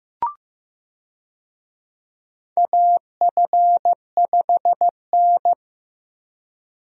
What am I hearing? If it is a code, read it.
Morse code, heard as a keyed tone, about 15 wpm, 700 Hz. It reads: AF5N
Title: USA Call Signs - 4 Characters - Repeated 2x - 15wpm